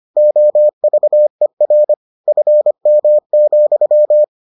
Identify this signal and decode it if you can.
Morse code, heard as a keyed tone, about 25 wpm, 600 Hz. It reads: OVER FM,